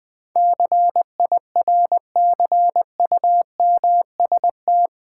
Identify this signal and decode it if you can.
Morse code, heard as a keyed tone, about 20 wpm, 700 Hz. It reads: CIRCUMST
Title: Build A Word - 4 Letters At A Time - 12-Letter Words - 20wpm